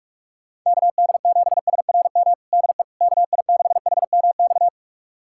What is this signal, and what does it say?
Morse code, heard as a keyed tone, about 46 wpm, 700 Hz. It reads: KD7SRM DE KI6HMX
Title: USA Callsign DE Callsign 45wpm